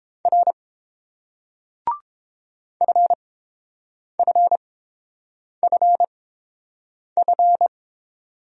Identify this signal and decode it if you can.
Morse code, heard as a keyed tone, about 29 wpm, spaced as for 7 wpm, 700 Hz. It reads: R FFFF